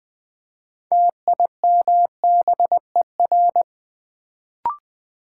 Morse code, heard as a keyed tone, about 20 wpm, 700 Hz. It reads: TIMBER
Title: Build A Word - 3 Letters At A Time - 6-Letter Words - 20wpm